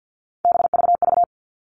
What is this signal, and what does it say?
Morse code, heard as a keyed tone, about 48 wpm, 700 Hz. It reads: BVV
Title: All Cut Numbers - 3-Digit 50wpm